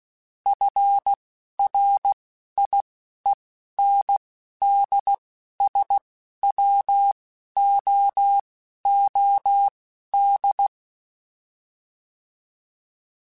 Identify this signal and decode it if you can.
Morse code, heard as a keyed tone, about 16 wpm, spaced as for 11 wpm, 800 Hz. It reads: FRIENDSWOOD